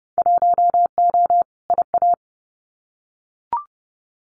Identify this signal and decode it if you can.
Morse code, heard as a keyed tone, about 30 wpm, 700 Hz. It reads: 1O IA
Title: Winter Field Day Exchanges Encore 30wpm